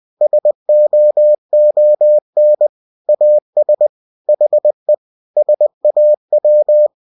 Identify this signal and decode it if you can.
Morse code, heard as a keyed tone, about 20 wpm, 600 Hz. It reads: SOON AS HE SAW
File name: cw-books / winnie-the-pooh / winnie-20wpm-page-0083